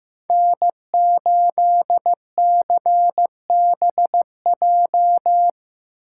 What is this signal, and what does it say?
Morse code, heard as a keyed tone, about 18 wpm, 700 Hz. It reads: N8CBJ
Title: Speed Racing - USA 1x3 Call Signs 15wpm